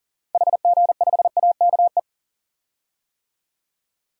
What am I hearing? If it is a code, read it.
Morse code, heard as a keyed tone, about 40 wpm, 700 Hz. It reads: HG5AKE
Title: DXCC-100 Call Signs - 6 Characters - 40wpm